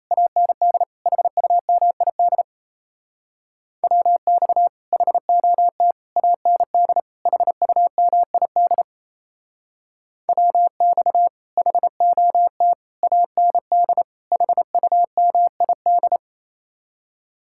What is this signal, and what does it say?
Morse code, heard as a keyed tone, about 37 wpm, 700 Hz. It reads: AND HUMID WX HOT AND HUMID WX HOT AND HUMID